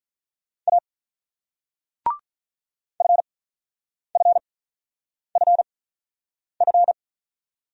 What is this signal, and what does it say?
Morse code, heard as a keyed tone, about 46 wpm, spaced as for 9 wpm, 700 Hz. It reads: A FFFF